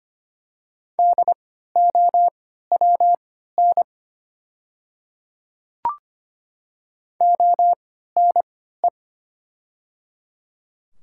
Morse code, heard as a keyed tone, about 25 wpm, spaced as for 14 wpm, 700 Hz. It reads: DOWN ONE
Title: 3x Character Spacing - Top 100 Words - 25wpm